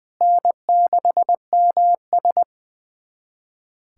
Morse code, heard as a keyed tone, about 20 wpm, 700 Hz. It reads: N6MS
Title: USA Callsign DE Callsign - 4 Characters - 20wpm